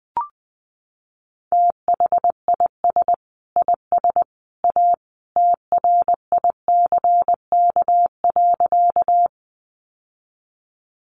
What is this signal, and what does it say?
Morse code, heard as a keyed tone, about 20 wpm, 700 Hz. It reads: THIS IS A TRICK.
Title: Sentences from Top 1500 Words 20wpm